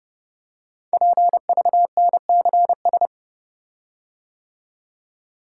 Morse code, heard as a keyed tone, about 30 wpm, 700 Hz. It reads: PVNCS
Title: Code Groups - 5 Letter:Numbers - 30wpm